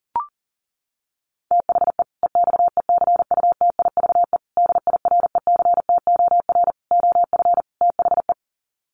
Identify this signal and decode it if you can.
Morse code, heard as a keyed tone, about 37 wpm, 700 Hz. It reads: THE EXECUTIVE DIRECTOR OF THE
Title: Sets of 5 Words Encore 40wpm